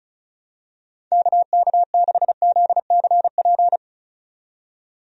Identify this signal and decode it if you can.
Morse code, heard as a keyed tone, about 34 wpm, 700 Hz. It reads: KK6ZCP
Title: USA Call Signs - 6 Characters - 35wpm